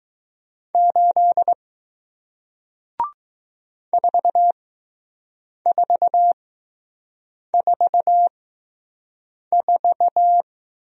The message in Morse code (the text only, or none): 8 4444